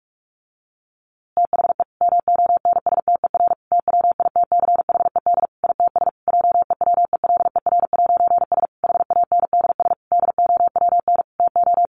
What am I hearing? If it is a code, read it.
Morse code, heard as a keyed tone, about 44 wpm, 700 Hz. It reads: THE MONSTER TWITCHED ITS JEWELER'S HANDS DOWN TO
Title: A Sound Of Thunder 45wpm